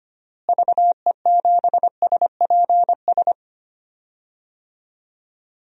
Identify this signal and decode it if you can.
Morse code, heard as a keyed tone, about 25 wpm, 700 Hz. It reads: VE7SPS
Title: DXCC-100 Call Signs - 6 Characters - Repeated 2x - 25wpm